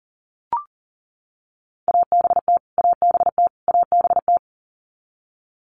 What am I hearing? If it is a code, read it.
Morse code, heard as a keyed tone, about 40 wpm, 700 Hz. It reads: ABT ABT ABT
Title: Abbreviations Repeated 3x - 40wpm